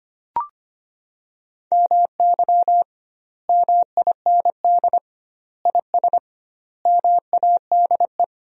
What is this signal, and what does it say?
Morse code, heard as a keyed tone, about 25 wpm, 700 Hz. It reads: MY MIND IS MADE
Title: 2x Word-Spacing - Sentences from Top 500 Words - 25wpm